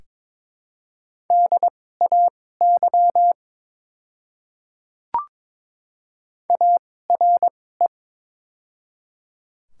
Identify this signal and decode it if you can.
Morse code, heard as a keyed tone, about 22 wpm, spaced as for 16 wpm, 700 Hz. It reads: DAY ARE
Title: 2x Character Spacing - Top 300 Words - 22wpm